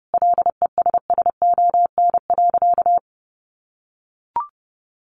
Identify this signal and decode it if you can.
Morse code, heard as a keyed tone, about 30 wpm, 700 Hz. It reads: LESSON.
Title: Sentences from Top 1200 Words 30wpm